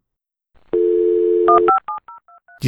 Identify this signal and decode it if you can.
DTMF, heard as telephone touch tones, tones of about 120 ms. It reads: 157023
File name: noises / gnse-001-modem-dialup-sound